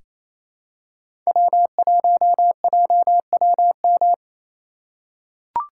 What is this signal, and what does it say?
Morse code, heard as a keyed tone, about 28 wpm, 700 Hz. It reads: W1JWM
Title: USA Call Signs - 5 Characters - 2x Character Spacing - 28wpm